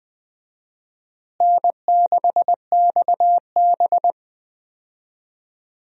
Morse code, heard as a keyed tone, about 21 wpm, 700 Hz. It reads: N6XB
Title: USA Call Signs - 4 Characters - Repeated 2x - 20wpm